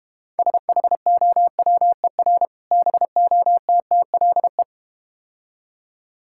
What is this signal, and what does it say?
Morse code, heard as a keyed tone, about 31 wpm, 700 Hz. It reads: SHOWER BOTTLE